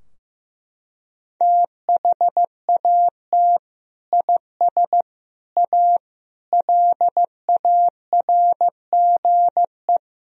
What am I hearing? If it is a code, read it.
Morse code, heard as a keyed tone, about 16 wpm, 700 Hz. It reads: THAT IS A LARGE
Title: Sentences from Top 1000 Words - Review Part 2 - 15wpm